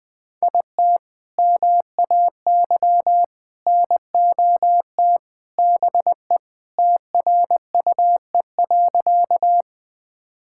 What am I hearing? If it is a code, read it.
Morse code, heard as a keyed tone, about 19 wpm, 700 Hz. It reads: IT MAY NOT BE TRUE.
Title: Sentences with 3-Word Phrases - Spoken Beginning - 20wpm